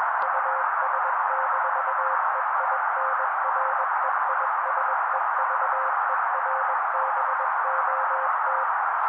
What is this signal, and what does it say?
Morse code, heard as a keyed tone, about 21 wpm, 530 Hz. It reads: US-EINREISEVERBOT
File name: daily / 20wpm noise